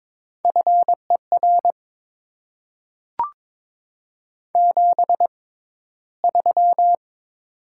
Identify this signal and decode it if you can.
Morse code, heard as a keyed tone, about 22 wpm, spaced as for 13 wpm, 690 Hz. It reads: FER 73
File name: Top 20 QSO Elements - 6x Character Spacing - 22wpm